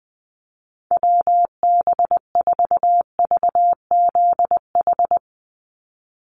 Morse code, heard as a keyed tone, about 20 wpm, 700 Hz. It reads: WB4VZH
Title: USA Call Signs 20wpm